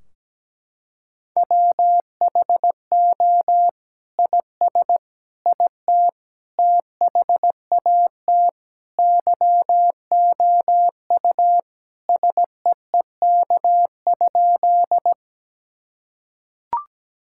Morse code, heard as a keyed tone, about 17 wpm, 700 Hz. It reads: WHO IS IT THAT YOU SEEK?
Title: Sentences from Top 1600 Words 17wpm